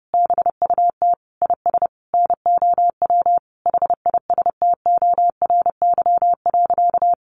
Morse code, heard as a keyed tone, about 30 wpm, 700 Hz. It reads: BUT IS NOW HISTORY.
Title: Sentences from Top 1500 Words 30wpm